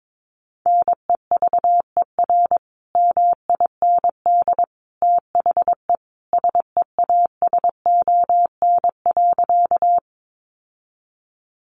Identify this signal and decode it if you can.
Morse code, heard as a keyed tone, about 22 wpm, 700 Hz. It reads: NEVER MIND THE SEASON.
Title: Sentences from Top 1400 Words 22wpm